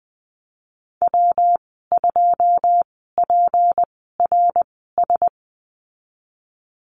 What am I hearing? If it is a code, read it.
Morse code, heard as a keyed tone, about 20 wpm, spaced as for 15 wpm, 700 Hz. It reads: W2PRS
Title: 186 Lesson 30g - Letter b - Call Signs 20wpm